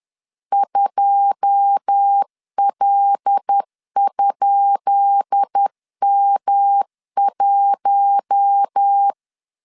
Morse code, heard as a keyed tone, about 10 wpm, 800 Hz. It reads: 2L?M1
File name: LEZIONI / 10 SIGNS / 10B INT 10wpm 2844